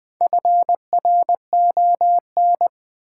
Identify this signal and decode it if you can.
Morse code, heard as a keyed tone, about 20 wpm, 700 Hz. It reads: FRON
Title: Build A Word - 1 Letter At A Time - 3 to 5-Letter Words - 20wpm